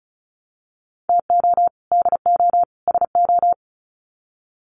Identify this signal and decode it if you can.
Morse code, heard as a keyed tone, about 34 wpm, 700 Hz. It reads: TO DO SO
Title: Sets of 3 Words 35wpm